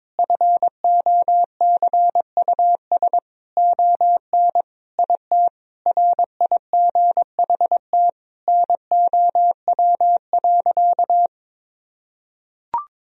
Morse code, heard as a keyed tone, about 22 wpm, 700 Hz. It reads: FOCUS ON IT RIGHT NOW.
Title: Sentences from Top 1000 Words - Review Part 4 - 22wpm